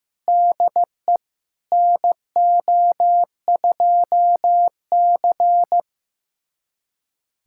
Morse code, heard as a keyed tone, about 15 wpm, 700 Hz. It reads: DE NO2C
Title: USA Callsign DE Callsign - 4 Characters - 15wpm